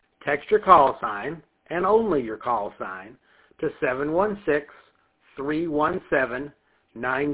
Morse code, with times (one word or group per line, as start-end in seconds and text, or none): none